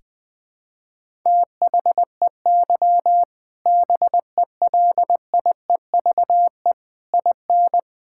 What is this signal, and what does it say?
Morse code, heard as a keyed tone, about 20 wpm, 700 Hz. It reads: THEY BELIEVE IN
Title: Sentences from Top 1000 Words - Review Part 5 - 20wpm